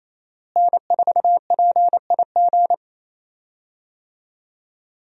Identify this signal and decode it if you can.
Morse code, heard as a keyed tone, about 28 wpm, 700 Hz. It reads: N4PIG
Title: USA Call Signs - 5 Characters - Repeated 2x - 28wpm